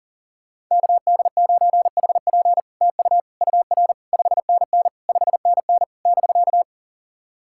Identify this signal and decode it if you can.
Morse code, heard as a keyed tone, about 40 wpm, 700 Hz. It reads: KD9HP TU UR 5NN 5NN <BK>
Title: SOTA QSO Practice 40wpm